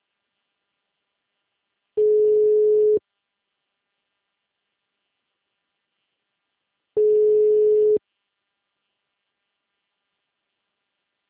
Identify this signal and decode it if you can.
Ringback tone (Europe, 425 Hz).
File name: out- -2041- - - .80118